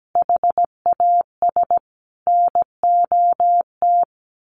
Morse code, heard as a keyed tone, about 17 wpm, 700 Hz. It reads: HAS NOT
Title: A Sound Of Thunder 17wpm